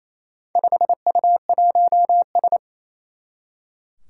Morse code, heard as a keyed tone, about 28 wpm, 700 Hz. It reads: 5U1S
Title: Four Challenging Letter-Number Combinations 28wpm